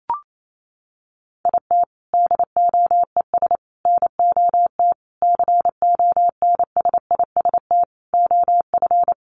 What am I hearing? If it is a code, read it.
Morse code, heard as a keyed tone, about 28 wpm, 700 Hz. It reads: IT DOES NOT CONSIST OF
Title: Sentences from Top 1400 Words 28wpm